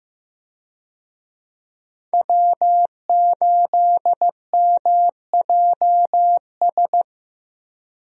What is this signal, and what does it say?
Morse code, heard as a keyed tone, about 15 wpm, 700 Hz. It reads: W8MJS
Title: USA Call Signs - 5 Characters - Repeated 2x - 15wpm